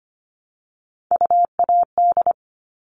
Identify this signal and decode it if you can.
Morse code, heard as a keyed tone, about 25 wpm, 700 Hz. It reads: UAD
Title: All Cut Numbers - 3-Digit 25wpm